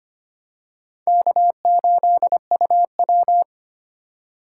Morse code, heard as a keyed tone, about 25 wpm, 700 Hz. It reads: K8UW
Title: USA Call Signs - 4 Characters - 2x Character Spacing - 25wpm